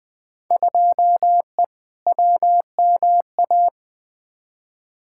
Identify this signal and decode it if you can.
Morse code, heard as a keyed tone, about 20 wpm, 700 Hz. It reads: 2E WMA
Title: Field Day Exchanges Encore - 20wpm